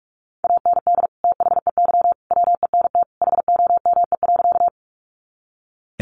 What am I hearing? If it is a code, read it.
Morse code, heard as a keyed tone, about 44 wpm, 700 Hz. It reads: AND THEY WENT HOME.
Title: Sentences from Top 200 Words 45wpm